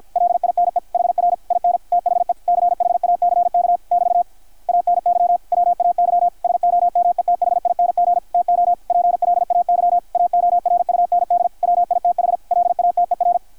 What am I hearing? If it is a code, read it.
Morse code, heard as a keyed tone, about 53 wpm, 720 Hz, with radio noise. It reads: LINE HR IN THE CHACK <BT> ANY WAY SOMETHING TO PLAY AORUND WITH LATER
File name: archived / CRSnet